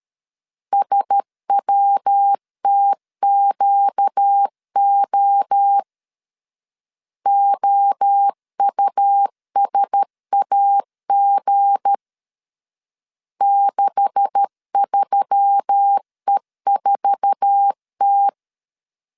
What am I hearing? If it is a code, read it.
Morse code, heard as a keyed tone, about 13 wpm, 800 Hz. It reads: SWTQO OUSAG 63E4T
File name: CW / DET / 1 9 1500C S B 12wpm 3819